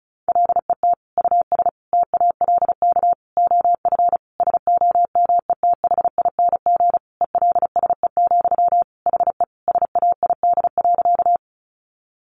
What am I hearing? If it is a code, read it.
Morse code, heard as a keyed tone, about 34 wpm, 700 Hz. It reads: LET US TALK OF SOMETHING ELSE, HE SAID.
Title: Sentences from Top 800 Words 35wpm